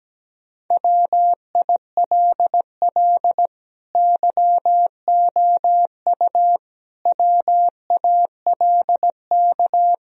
Morse code, heard as a keyed tone, about 17 wpm, 700 Hz. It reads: WILL YOU WALK
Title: Sentences from Top 1000 Words - Review Part 1 - 17wpm